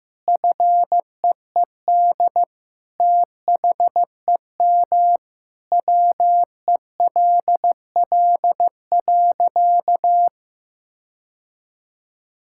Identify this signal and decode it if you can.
Morse code, heard as a keyed tone, about 16 wpm, 700 Hz. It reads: FEED THEM WELL.
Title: Sentences from Top 1600 Words 15wpm